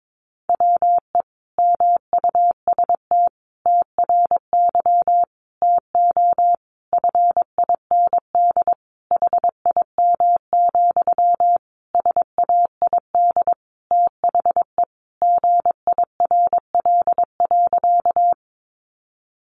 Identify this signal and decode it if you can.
Morse code, heard as a keyed tone, about 22 wpm, 700 Hz. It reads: WE MUST TRY TO FIND HIM, SAID THE GIRL.
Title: Sentences from Top 400 Words 22wpm